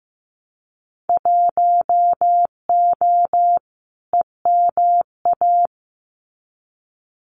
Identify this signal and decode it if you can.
Morse code, heard as a keyed tone, about 15 wpm, 700 Hz. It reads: 1O EMA
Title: Winter Field Day Exchanges Encore 15wpm